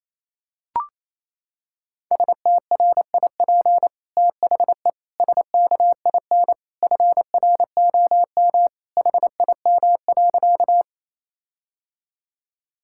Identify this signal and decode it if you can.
Morse code, heard as a keyed tone, about 28 wpm, 700 Hz. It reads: STRIP THE SKIN FROM HIM.
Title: Sentences from Top 1600 Words 28wpm